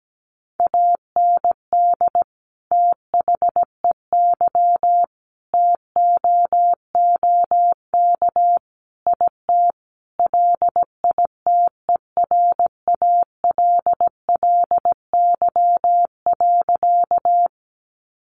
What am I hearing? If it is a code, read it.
Morse code, heard as a keyed tone, about 17 wpm, 700 Hz. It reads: AND THEY TOOK IT LITERALLY.
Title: Sentences from Top 1400 Words 17wpm